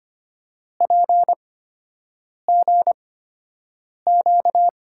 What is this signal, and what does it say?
Morse code, heard as a keyed tone, about 25 wpm, spaced as for 7 wpm, 700 Hz. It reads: PGQ
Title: USA Call Signs - 6 Characters - 8x Character Spacing - 25wpm